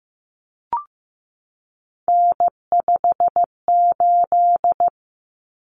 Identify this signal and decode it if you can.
Morse code, heard as a keyed tone, about 15 wpm, 700 Hz. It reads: N58
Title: Common Cut Numbers - 3-Digit 15wpm